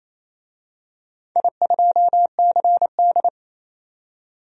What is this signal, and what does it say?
Morse code, heard as a keyed tone, about 28 wpm, 700 Hz. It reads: I2CD
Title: DXCC-100 Call Signs - 4 Characters - 28wpm